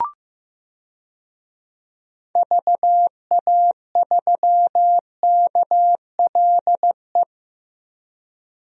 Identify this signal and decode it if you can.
Morse code, heard as a keyed tone, about 15 wpm, 700 Hz. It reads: VA3KLE
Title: DXCC-100 Call Signs - 6 Characters - Repeated 3x - 15wpm